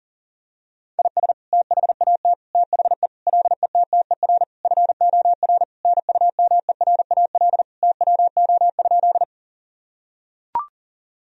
Morse code, heard as a keyed tone, about 40 wpm, 700 Hz. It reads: IS THAT THE LETTER FOR NUMERAL TWO?
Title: Sentences from Top 1000 Words - Review Part 2 - 40wpm